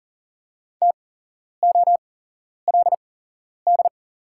Morse code, heard as a keyed tone, about 40 wpm, spaced as for 11 wpm, 700 Hz. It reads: TOLD
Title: 8x Character Spacing - Top 500 Words - 40wpm